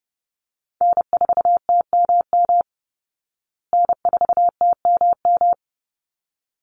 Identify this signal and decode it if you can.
Morse code, heard as a keyed tone, about 30 wpm, 700 Hz. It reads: N4TMM N4TMM
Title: Call Signs Repeated 3x - 30wpm